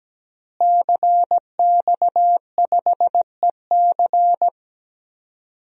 Morse code, heard as a keyed tone, about 17 wpm, 700 Hz. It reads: CX5EC
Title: DXCC-100 Call Signs - 5 Characters - Repeated 3x - 17wpm